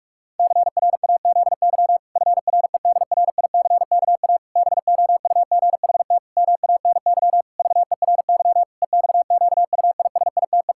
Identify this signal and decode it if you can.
Morse code, heard as a keyed tone, about 41 wpm, 700 Hz. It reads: CRAZY FREDRICKA BOUGHT MANY VERY EXQUISITE
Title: Pangram Challenge v2 - 45wpm